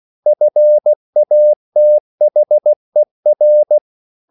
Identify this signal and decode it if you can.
Morse code, heard as a keyed tone, about 16 wpm, 600 Hz. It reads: FATHER
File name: CW / Practice / EnglishQuotations0416